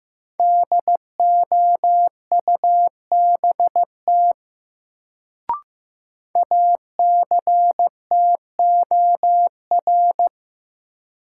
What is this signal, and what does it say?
Morse code, heard as a keyed tone, about 15 wpm, 700 Hz. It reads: DOUBT ACTOR
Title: Sentences from Top 1500 Words - New Word Drill - 15wpm